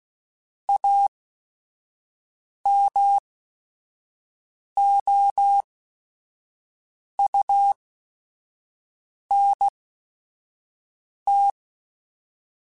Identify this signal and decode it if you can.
Morse code, heard as a keyed tone, about 16 wpm, spaced as for 5 wpm, 800 Hz. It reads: AMOUNT
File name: Technical Subjects / Technical Subjects-mp3s / XL @5